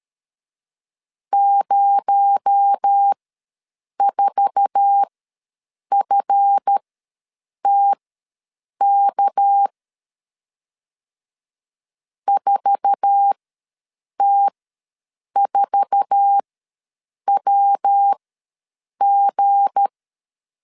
Morse code, heard as a keyed tone, about 13 wpm, spaced as for 7 wpm, 800 Hz. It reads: 04FTK 4T4WG
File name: LEZIONI / 8 YJ45 / 8 YJ4 ws 12wpm 3249